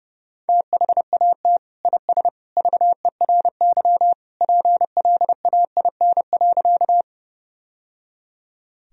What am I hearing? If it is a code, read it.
Morse code, heard as a keyed tone, about 30 wpm, 700 Hz. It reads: THAT IS VERY PLAIN.
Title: Sentences from Top 1000 Words - Review Part 2 - 30wpm